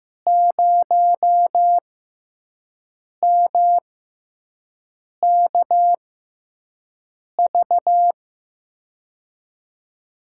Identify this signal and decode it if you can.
Morse code, heard as a keyed tone, about 15 wpm, spaced as for 5 wpm, 700 Hz. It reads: 0MKV
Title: USA Call Signs - 6 Characters - 6x Character Spacing - 15wpm